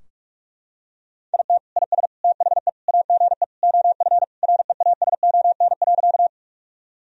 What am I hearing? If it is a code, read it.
Morse code, heard as a keyed tone, about 41 wpm, 700 Hz. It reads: IT IS THE AGE OF REASON.
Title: Sentences from Top 1000 Words - Review Part 3 - 45wpm